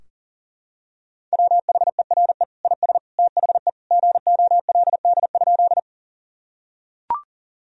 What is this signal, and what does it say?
Morse code, heard as a keyed tone, about 40 wpm, 700 Hz. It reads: WHERE IS THE GOLD?
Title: Sentences from Top 1000 Words - Review Part 3 - 40wpm